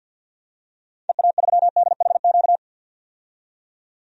Morse code, heard as a keyed tone, about 51 wpm, 700 Hz. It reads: EA3DHQ
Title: DXCC-100 Call Signs - 6 Characters - 50wpm